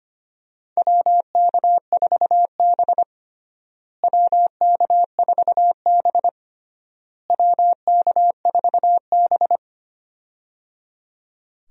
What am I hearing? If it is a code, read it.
Morse code, heard as a keyed tone, about 25 wpm, 700 Hz. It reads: WK4B WK4B WK4B
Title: USA Call Signs - 4 Characters - Repeated 3x - 25wpm